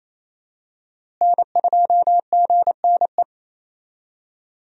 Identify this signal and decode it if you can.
Morse code, heard as a keyed tone, about 28 wpm, 700 Hz. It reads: N2GNE
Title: USA Call Signs - 5 Characters - Repeated 3x - 28wpm